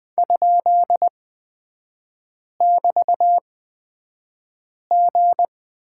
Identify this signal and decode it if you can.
Morse code, heard as a keyed tone, about 20 wpm, spaced as for 5 wpm, 700 Hz. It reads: ?<BT>G
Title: 255 Lesson 41e - Period - Three Characters 20wpm